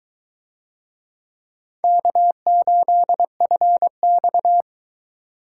Morse code, heard as a keyed tone, about 20 wpm, 700 Hz. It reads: K8FX